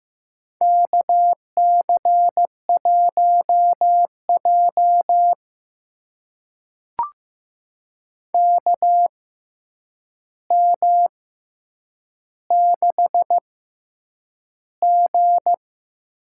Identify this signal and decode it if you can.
Morse code, heard as a keyed tone, about 15 wpm, spaced as for 8 wpm, 700 Hz. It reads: KC1J KM6G
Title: USA Call Signs - 6x Character Spacing - 15wpm